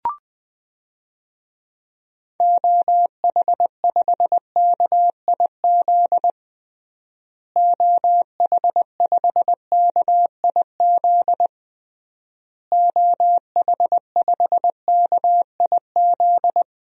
Morse code, heard as a keyed tone, about 21 wpm, 700 Hz. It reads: OH5KIZ OH5KIZ OH5KIZ
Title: DXCC-100 Call Signs - 6 Characters - Repeated 3x - 20wpm